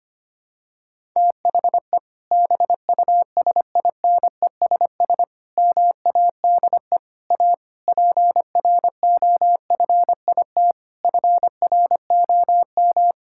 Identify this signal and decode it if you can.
Morse code, heard as a keyed tone, about 25 wpm, 700 Hz. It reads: THE BUSINESS MADE A PROFIT FROM
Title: Sentences from Top 1000 Words - Review Part 5 - 25wpm